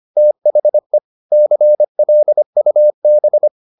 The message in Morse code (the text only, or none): THE CLUB